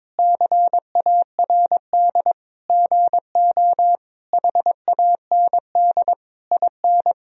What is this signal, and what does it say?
Morse code, heard as a keyed tone, about 22 wpm, 700 Hz. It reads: CARD GO HAND IN
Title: Sentences from Top 1000 Words - Review Part 4 - 22wpm